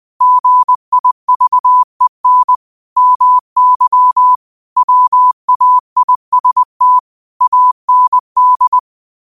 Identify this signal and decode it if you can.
Morse code, heard as a keyed tone, about 20 wpm, 1000 Hz. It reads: GIVEN MY WAIST AND
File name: qotdfiles / 20wpm